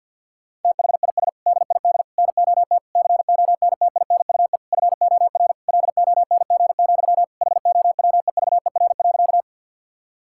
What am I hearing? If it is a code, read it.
Morse code, heard as a keyed tone, about 50 wpm, 700 Hz. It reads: THIS DID NOT CONTINUE FOR LONG, HOWEVER.
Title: Sentences from Top 1000 Words - Review Part 6 - 50wpm